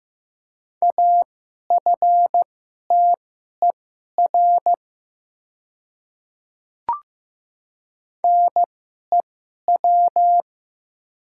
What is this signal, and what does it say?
Morse code, heard as a keyed tone, about 15 wpm, spaced as for 11 wpm, 700 Hz. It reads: AFTER NEW
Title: 2x Character Spacing - Top 300 Words - 15wpm